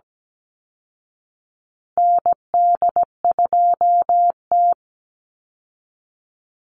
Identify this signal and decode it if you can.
Morse code, heard as a keyed tone, about 17 wpm, 700 Hz. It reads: ND2T
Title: Call Signs Repeated 2x - 17wpm